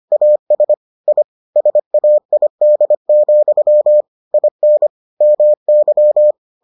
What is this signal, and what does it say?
Morse code, heard as a keyed tone, about 25 wpm, 600 Hz. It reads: AS I SAID, IN MY